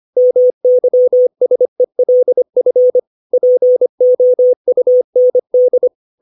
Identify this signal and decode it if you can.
Morse code, heard as a keyed tone, about 25 wpm, 500 Hz. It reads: MYSELF POUND
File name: 632 words / 632 Words 25WPM